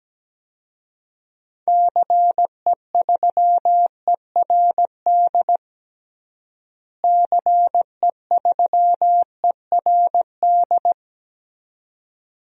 Morse code, heard as a keyed tone, about 17 wpm, 700 Hz. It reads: CE3ERD CE3ERD
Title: DXCC-100 Call Signs - 6 Characters - Repeated 2x - 17wpm